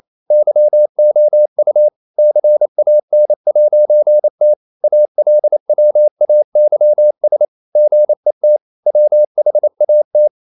Morse code, heard as a keyed tone, about 28 wpm, 600 Hz. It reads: YOU CAN'T ALWAYS GET WHAT
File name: CW / Practice / EnglishSayings28